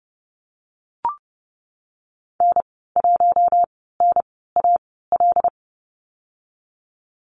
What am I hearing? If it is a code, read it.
Morse code, heard as a keyed tone, about 32 wpm, 700 Hz. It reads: N 1 N A L
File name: Lesson 19g - Letter w - Call Signs 30wpm